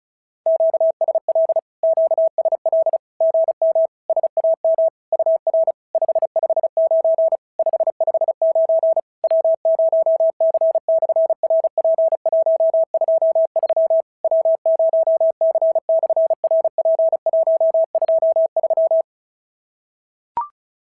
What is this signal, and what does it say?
Morse code, heard as a keyed tone, about 36 wpm, 650 Hz. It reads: QSL QSL GM SAM UR 559 559 W0C/RP123 W0C/RP123
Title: SOTA QSO Practice 35wpm